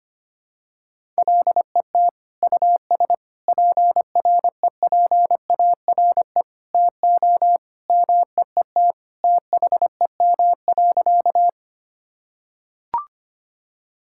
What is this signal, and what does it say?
Morse code, heard as a keyed tone, about 25 wpm, 700 Hz. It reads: LET US PREPARE TO MEET THEM.
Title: Sentences from Top 1000 Words - Review Part 6 - 25wpm